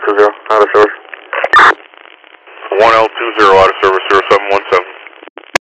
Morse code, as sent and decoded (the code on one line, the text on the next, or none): none